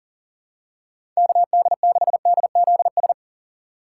Morse code, heard as a keyed tone, about 40 wpm, 700 Hz. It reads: KD6DZS
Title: USA Call Signs - 6 Characters - 4x Character Spacing - 40wpm